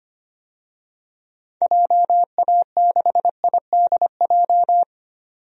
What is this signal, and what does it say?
Morse code, heard as a keyed tone, about 25 wpm, 700 Hz. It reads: JA6IDJ
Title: DXCC-100 Call Signs - 6 Characters - 25wpm